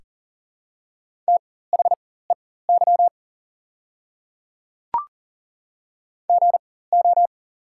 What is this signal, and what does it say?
Morse code, heard as a keyed tone, about 37 wpm, spaced as for 18 wpm, 700 Hz. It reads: THEY GO
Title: 4x Character Spacing - Top 300 Words - 40wpm